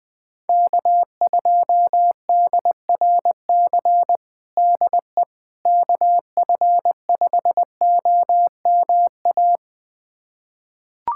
Morse code, heard as a keyed tone, about 20 wpm, 700 Hz. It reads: K2DRC DE KF5OMA
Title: USA Callsign DE Callsign 20wpm